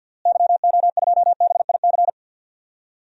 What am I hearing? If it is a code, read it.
Morse code, heard as a keyed tone, about 51 wpm, 700 Hz. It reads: YO2BIC
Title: DXCC-100 Call Signs - 6 Characters - Repeated 3x - 50wpm